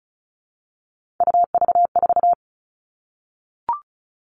Morse code, heard as a keyed tone, about 34 wpm, 700 Hz. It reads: UV4
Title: All Cut Numbers - 3-Digit 35wpm